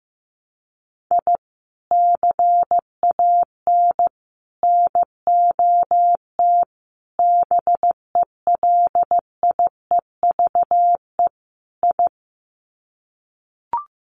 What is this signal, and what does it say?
Morse code, heard as a keyed tone, about 15 wpm, 700 Hz. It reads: I CAN NOT BELIEVE I
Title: Sets of 5 Words Encore2 15wpm